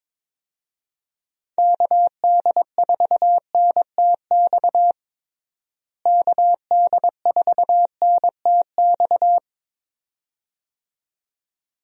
Morse code, heard as a keyed tone, about 22 wpm, 700 Hz. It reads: KD4NTX KD4NTX
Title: USA Call Signs - 6 Characters - Repeated 2x - 22wpm